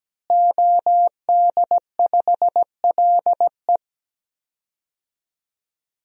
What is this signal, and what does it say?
Morse code, heard as a keyed tone, about 17 wpm, 700 Hz. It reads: OD5LE
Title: DXCC-100 Call Signs - 5 Characters - Repeated 2x - 17wpm